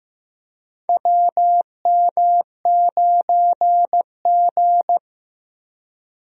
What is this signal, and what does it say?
Morse code, heard as a keyed tone, about 15 wpm, 700 Hz. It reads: WM9G
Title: USA Call Signs - 4 Characters - Repeated 3x - 15wpm